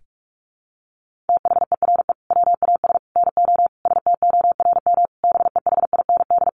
Morse code, heard as a keyed tone, about 44 wpm, 700 Hz. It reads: THERE WAS NO STORM BEHIND
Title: Sentences from Top 1500 Words 45wpm